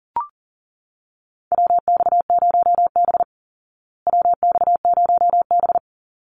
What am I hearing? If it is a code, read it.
Morse code, heard as a keyed tone, about 40 wpm, 700 Hz. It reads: WX0B WX0B
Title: Call Signs Repeated 3x - 40wpm